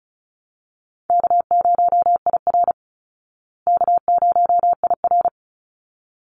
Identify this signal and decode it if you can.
Morse code, heard as a keyed tone, about 34 wpm, 700 Hz. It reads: K0IR K0IR
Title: Easy Call Signs Repeated 2x - 35wpm